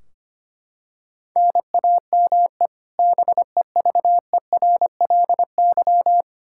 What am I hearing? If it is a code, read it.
Morse code, heard as a keyed tone, about 25 wpm, 700 Hz. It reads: NAME BEVERLY
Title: Practice QSO - Exchange 1 - Beginning - 25wpm